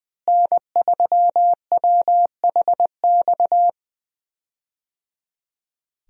Morse code, heard as a keyed tone, about 20 wpm, 700 Hz. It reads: N3WHX
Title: USA Call Signs - 5 Characters - Repeated 3x - 20wpm